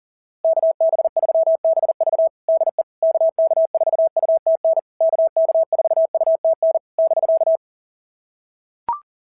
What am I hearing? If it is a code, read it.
Morse code, heard as a keyed tone, about 39 wpm, 660 Hz. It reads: KB3BV DE KK4UTN KK4UTN <BK>
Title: SOTA QSO Practice 40wpm